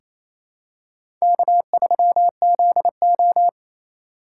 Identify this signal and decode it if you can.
Morse code, heard as a keyed tone, about 27 wpm, 700 Hz. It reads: K3ZO
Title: USA Call Signs - 4 Characters - 4x Character Spacing - 28wpm